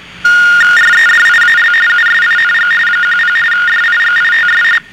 RTTY, about 45.45 baud. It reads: REDIAL USING 10 DIGITS